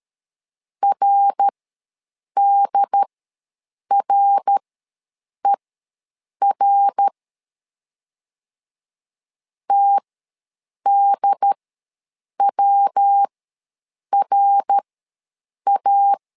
Morse code, heard as a keyed tone, about 13 wpm, spaced as for 7 wpm, 800 Hz. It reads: RDRER TDWRA
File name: LEZIONI / 3 MWRO / 3 MWR ws 12wpm 1809